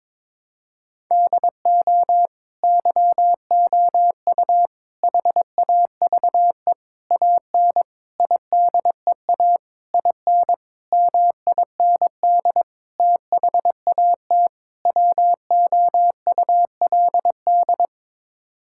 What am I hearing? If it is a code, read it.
Morse code, heard as a keyed tone, about 22 wpm, 700 Hz. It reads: DO YOU HAVE AN IDEA IN MIND THAT WOULD
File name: Sets of 9 Words - 22wpm